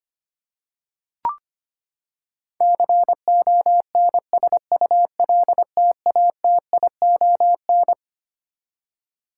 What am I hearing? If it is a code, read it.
Morse code, heard as a keyed tone, about 25 wpm, 700 Hz. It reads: CONSULTATION
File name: Twelve Letter Words - 25wpm